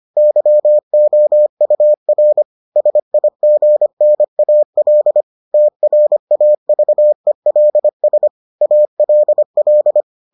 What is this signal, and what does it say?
Morse code, heard as a keyed tone, about 25 wpm, 600 Hz. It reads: YOUR SIGNAL TRAVELS ALL